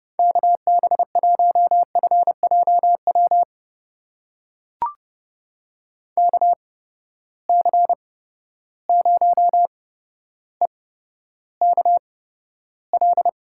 Morse code, heard as a keyed tone, about 31 wpm, spaced as for 13 wpm, 700 Hz. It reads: KB1FJW KC0EKL
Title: USA Call Signs - 6 Characters - 8x Character Spacing - 30wpm